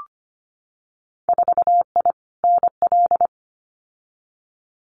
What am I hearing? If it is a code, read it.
Morse code, heard as a keyed tone, about 25 wpm, 690 Hz. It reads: 4I NL
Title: Winter Field Day Exchanges Encore 25wpm